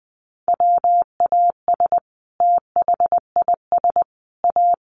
Morse code, heard as a keyed tone, about 20 wpm, 700 Hz. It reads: WAS THIS A
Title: Sentences from Top 1200 Words 20wpm